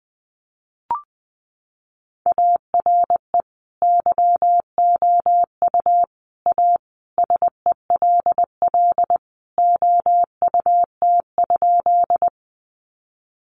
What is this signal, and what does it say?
Morse code, heard as a keyed tone, about 19 wpm, 700 Hz. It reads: ARE YOU A SELL OUT?
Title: Sentences from Top 1100 Words 20wpm